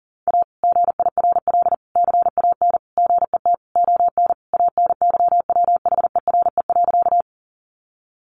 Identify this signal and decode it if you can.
Morse code, heard as a keyed tone, about 40 wpm, 700 Hz. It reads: A GIRL CAN GET ON ANYWHERE.
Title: Sentences from Top 1400 Words 40wpm